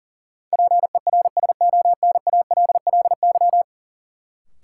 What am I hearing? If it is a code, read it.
Morse code, heard as a keyed tone, about 40 wpm, 700 Hz. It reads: PERSONALLY
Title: Sentences from Top 1400 Words - New Word Drill Alt - 40wpm